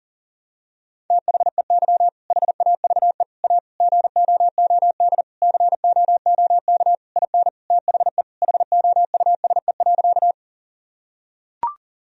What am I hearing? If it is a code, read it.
Morse code, heard as a keyed tone, about 40 wpm, 700 Hz. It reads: THEY HAVE A GOOD COOK IN THE HOUSE.
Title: Sentences from Top 1000 Words - Review Part 4 - 40wpm